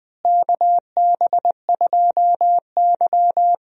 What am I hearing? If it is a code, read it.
Morse code, heard as a keyed tone, about 20 wpm, 700 Hz. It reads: KB2Y
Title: USA Call Signs - 4 Characters - Repeated 3x - 20wpm